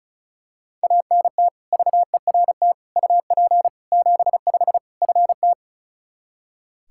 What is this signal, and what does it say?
Morse code, heard as a keyed tone, about 34 wpm, 700 Hz. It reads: ANT VERT UP 75 FT
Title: Practice QSO - Exchange 2 - Info - 35wpm